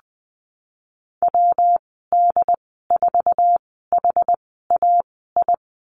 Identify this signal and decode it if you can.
Morse code, heard as a keyed tone, about 20 wpm, spaced as for 15 wpm, 700 Hz. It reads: WD4HAI
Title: 122 Lesson 20g - Number 3 - Call Signs 20wpm